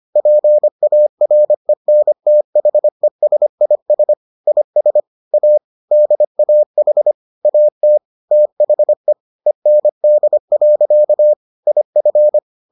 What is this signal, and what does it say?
Morse code, heard as a keyed tone, about 25 wpm, 600 Hz. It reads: PARENTHESIS IS A DAH AT THE END. IF